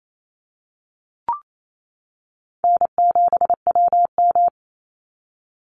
Morse code, heard as a keyed tone, about 28 wpm, 700 Hz. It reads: N7WM
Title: NA SOTA Chaser Callsigns 28wpm